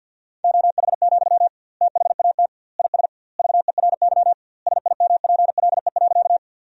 Morse code, heard as a keyed tone, about 52 wpm, 700 Hz. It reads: OH, THAT IS VERY SIMPLE.
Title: 2x Word-Spacing - Sentences from Top 500 Words - 50wpm